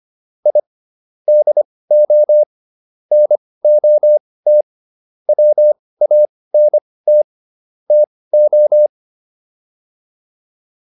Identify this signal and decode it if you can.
Morse code, heard as a keyed tone, about 25 wpm, spaced as for 18 wpm, 600 Hz. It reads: I DO NOT WANT TO